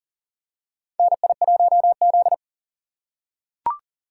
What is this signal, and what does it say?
Morse code, heard as a keyed tone, about 40 wpm, 700 Hz. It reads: NI1Z
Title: USA Call Signs - 4x Character Spacing - 40wpm